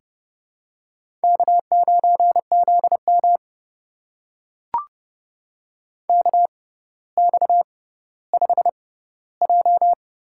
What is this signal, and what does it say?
Morse code, heard as a keyed tone, about 30 wpm, spaced as for 15 wpm, 700 Hz. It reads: K9ZM KX5J
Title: USA Call Signs - 4 Characters - 6x Character Spacing - 30wpm